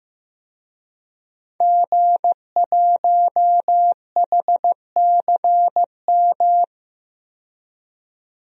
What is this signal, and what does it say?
Morse code, heard as a keyed tone, about 16 wpm, 700 Hz. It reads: G1HCM
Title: DXCC-100 Call Signs - 5 Characters - 15wpm